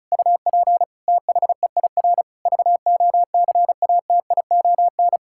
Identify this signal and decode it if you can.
Morse code, heard as a keyed tone, about 36 wpm, 700 Hz. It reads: UP THEIR VOCATION
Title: Pangram Challenge v2 - 35wpm